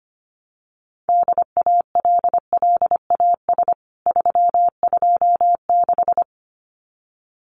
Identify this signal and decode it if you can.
Morse code, heard as a keyed tone, about 25 wpm, 700 Hz. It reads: DALLAS 326
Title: CWT Exchange 25wpm